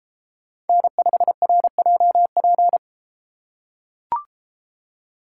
Morse code, heard as a keyed tone, about 32 wpm, 700 Hz. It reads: N5RJP